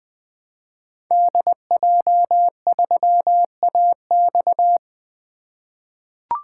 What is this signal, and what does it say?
Morse code, heard as a keyed tone, about 21 wpm, 700 Hz. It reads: DJ3AX
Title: 2023 SOTA Activator Call Signs - 20wpm